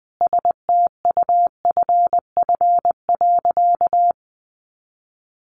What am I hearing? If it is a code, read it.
Morse code, heard as a keyed tone, about 20 wpm, 700 Hz. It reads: STUFF.
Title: Sentences from Top 1400 Words 20wpm